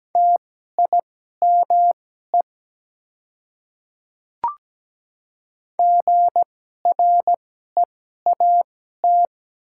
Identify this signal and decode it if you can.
Morse code, heard as a keyed tone, about 17 wpm, spaced as for 12 wpm, 700 Hz. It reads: TIME GREAT
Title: 2x Character Spacing - Top 200 Words - 17wpm